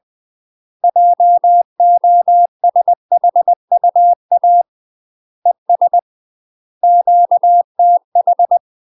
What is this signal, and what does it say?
Morse code, heard as a keyed tone, about 20 wpm, 700 Hz. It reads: JOSHUA ES QTH
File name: UK MASTER QSOs TRAINING / 20WPM / UK MASTER QSOs 0034